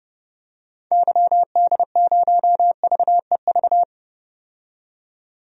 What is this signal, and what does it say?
Morse code, heard as a keyed tone, about 30 wpm, 700 Hz. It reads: YD0VEV
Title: DXCC-100 Call Signs - 6 Characters - 30wpm